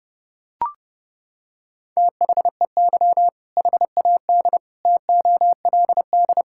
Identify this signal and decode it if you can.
Morse code, heard as a keyed tone, about 30 wpm, 700 Hz. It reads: THEY HAD TOLD
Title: Sentences from Top 1600 Words 30wpm